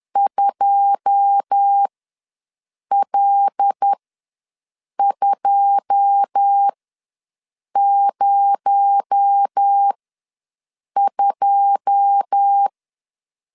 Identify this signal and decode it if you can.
Morse code, heard as a keyed tone, about 10 wpm, spaced as for 6 wpm, 800 Hz. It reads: 2L202
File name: LEZIONI / 9 2 ws 10wpm 4130